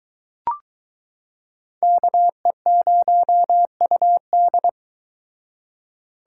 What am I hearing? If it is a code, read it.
Morse code, heard as a keyed tone, about 23 wpm, 700 Hz. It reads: KE0UD